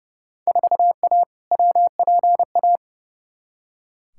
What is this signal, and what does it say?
Morse code, heard as a keyed tone, about 31 wpm, 700 Hz. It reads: 4A WPA
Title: Field Day Exchanges Encore - 30wpm